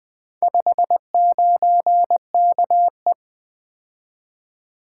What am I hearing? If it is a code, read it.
Morse code, heard as a keyed tone, about 20 wpm, 700 Hz. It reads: 59KE
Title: Four Letter-Number Combinations - 20wpm